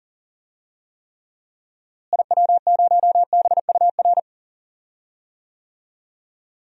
Morse code, heard as a keyed tone, about 40 wpm, 700 Hz. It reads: IW0BUR
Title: DXCC-100 Call Signs - 6 Characters - Repeated 3x - 40wpm